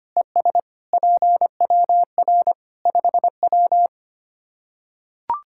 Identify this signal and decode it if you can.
Morse code, heard as a keyed tone, about 25 wpm, 700 Hz. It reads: ES PWR 5W
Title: Practice QSO - Exchange 2 - Info - 25wpm